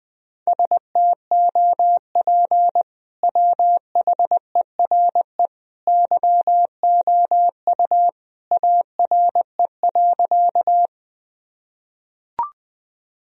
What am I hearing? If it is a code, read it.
Morse code, heard as a keyed tone, about 21 wpm, 700 Hz. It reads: STOP WHERE YOU ARE.
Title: Sentences from Top 1000 Words - Review Part 1 - 20wpm